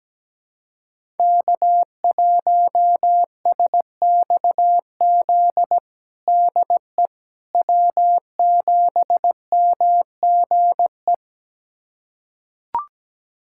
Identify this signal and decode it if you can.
Morse code, heard as a keyed tone, about 17 wpm, 700 Hz. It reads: K1SXZ DE W7MGE
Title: USA Callsign DE Callsign - 5 Characters - 17wpm